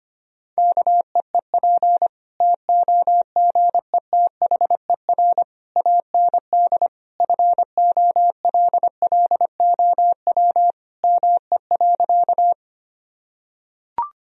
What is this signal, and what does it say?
Morse code, heard as a keyed tone, about 25 wpm, 700 Hz. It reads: KEEP TOGETHER AND FOLLOW ME.
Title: Sentences from Top 1000 Words - Review Part 1 - 25wpm